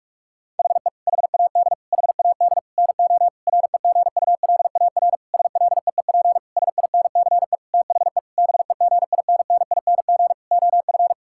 Morse code, heard as a keyed tone, about 41 wpm, 700 Hz. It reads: HE HAD HAD NO REGULAR SLEEP SINCE THE BEGINNING OF
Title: Sentences from Top 1000 Words - Review Part 6 - 45wpm